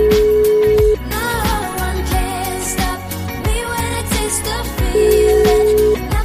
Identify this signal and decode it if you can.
Ringback tone (Europe, 425 Hz), with noise.